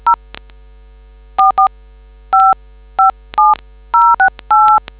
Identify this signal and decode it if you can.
DTMF, heard as telephone touch tones, tones of about 140 ms. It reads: 0 44557068